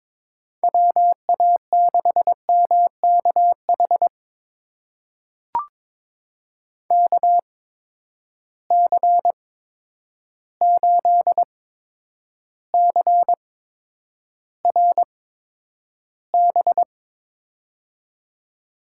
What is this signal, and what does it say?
Morse code, heard as a keyed tone, about 21 wpm, spaced as for 9 wpm, 700 Hz. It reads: WA6MKH KC8CRB
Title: USA Call Signs - 6 Characters - 8x Character Spacing - 22wpm